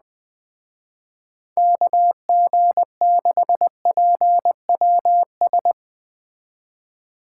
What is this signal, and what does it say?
Morse code, heard as a keyed tone, about 21 wpm, 700 Hz. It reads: KG6PWS